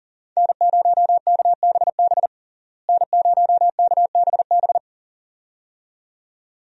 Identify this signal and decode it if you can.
Morse code, heard as a keyed tone, about 40 wpm, 700 Hz. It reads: N0KBB N0KBB
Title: USA Call Signs - 5 Characters - Repeated 2x - 40wpm